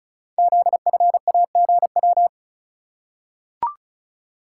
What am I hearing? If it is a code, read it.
Morse code, heard as a keyed tone, about 36 wpm, 700 Hz. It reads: ZFAGW
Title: Code Groups - 5 Letters - 35wpm